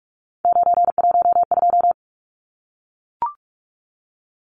Morse code, heard as a keyed tone, about 44 wpm, 700 Hz. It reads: 912
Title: Numbers - 3-Digit 45wpm